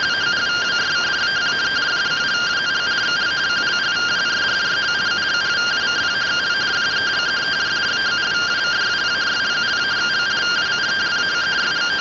Two-tone teleprinter signal, about 170 Hz shift.
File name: findings / CT AXIAL -256.gen-29 rtty rx